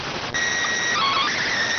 SSTV (slow-scan television), PD 290.